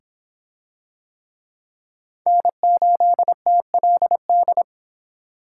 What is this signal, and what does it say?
Morse code, heard as a keyed tone, about 24 wpm, 700 Hz. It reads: N8TLD